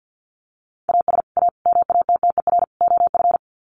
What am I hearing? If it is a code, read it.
Morse code, heard as a keyed tone, about 50 wpm, 700 Hz. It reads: AS A MATTER OF
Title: Sets of 4 Words 50wpm